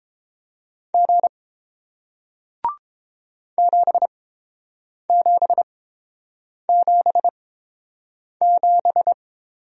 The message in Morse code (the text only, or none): G 7777